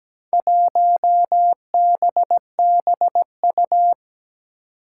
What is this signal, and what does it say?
Morse code, heard as a keyed tone, about 17 wpm, 700 Hz. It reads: 1BBU
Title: Four Challenging Letter-Number Combinations 17wpm